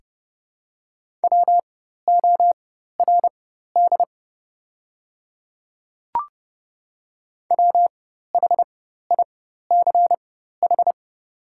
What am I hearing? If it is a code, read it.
Morse code, heard as a keyed tone, about 30 wpm, spaced as for 14 wpm, 700 Hz. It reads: WORD WHICH
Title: 4x Character Spacing - Top 100 Words - 30wpm